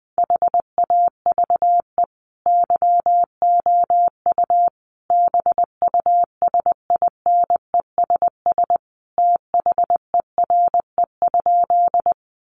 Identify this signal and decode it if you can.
Morse code, heard as a keyed tone, about 19 wpm, 700 Hz. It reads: HAVE YOU BUSINESS THERE?
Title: Sentences from Top 600 Words 20wpm